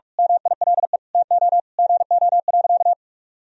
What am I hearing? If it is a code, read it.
Morse code, heard as a keyed tone, about 46 wpm, 700 Hz. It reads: MILE TO GO.
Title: Sentences from Top 1000 Words - Review Part 1 - 45wpm